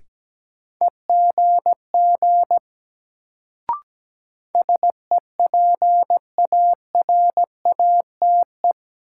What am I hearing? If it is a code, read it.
Morse code, heard as a keyed tone, about 17 wpm, 700 Hz. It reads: EGG SEPARATE
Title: Sentences from Top 1000 Words - Review Part 7 - New Word Drill - 17wpm